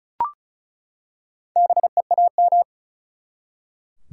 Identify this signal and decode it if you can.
Morse code, heard as a keyed tone, about 36 wpm, 700 Hz. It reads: BEAM
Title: Top 180 QSO Elements 35wpm